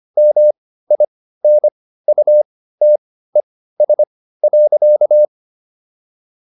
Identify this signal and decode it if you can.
Morse code, heard as a keyed tone, about 25 wpm, spaced as for 15 wpm, 600 Hz. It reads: MINUTES.